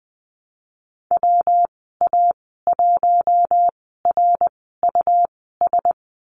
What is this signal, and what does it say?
Morse code, heard as a keyed tone, about 20 wpm, spaced as for 15 wpm, 700 Hz. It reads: WA1RUS
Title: 179 Lesson 29g - Slash - Call Signs 20wpm